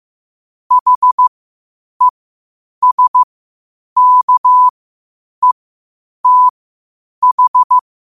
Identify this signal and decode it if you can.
Morse code, heard as a keyed tone, about 15 wpm, spaced as for 8 wpm, 1000 Hz. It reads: HESKETH
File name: qotdfiles / 8wpm